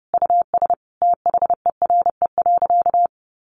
Morse code, heard as a keyed tone, about 30 wpm, 700 Hz. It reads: US THERE.
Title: Sentences from Top 1500 Words 30wpm